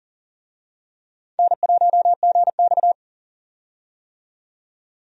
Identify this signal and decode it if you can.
Morse code, heard as a keyed tone, about 40 wpm, 700 Hz. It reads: N1GX
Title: USA Call Signs - 4 Characters - 40wpm